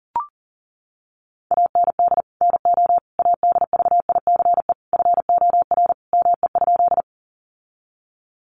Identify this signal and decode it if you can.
Morse code, heard as a keyed tone, about 40 wpm, 700 Hz. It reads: AND NO ADVICE FOR ME?
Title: Sentences from Top 1100 Words 40wpm